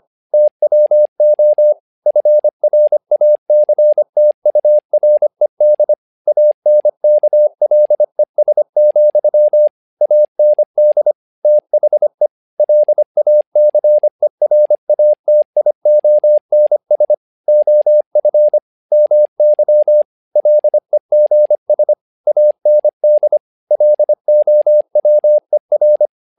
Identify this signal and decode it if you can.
Morse code, heard as a keyed tone, about 25 wpm, 600 Hz. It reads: TWO FRACTURED ANKLES, AND THE LACERATIONS OF MY LEGS AND LOWER
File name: learning center / tower trouble / tower trouble 25wpm